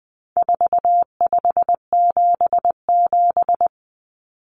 Morse code, heard as a keyed tone, about 20 wpm, 700 Hz. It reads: 4577
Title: Common Cut Numbers - 4-Digit 20wpm